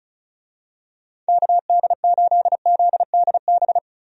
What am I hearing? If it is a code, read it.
Morse code, heard as a keyed tone, about 36 wpm, 700 Hz. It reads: KD8ZDB
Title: USA Call Signs - 6 Characters - Repeated 3x - 35wpm